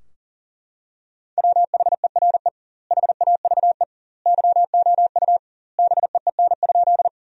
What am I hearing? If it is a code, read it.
Morse code, heard as a keyed tone, about 40 wpm, 700 Hz. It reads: WHERE HAVE YOU BEEN?
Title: 2x Word-Spacing - Sentences from Top 200 Words - 40wpm